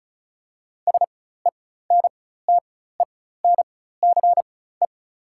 Morse code, heard as a keyed tone, about 35 wpm, spaced as for 16 wpm, 690 Hz. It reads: SENTENCE
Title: 4x Character Spacing - Top 500 Words - 35wpm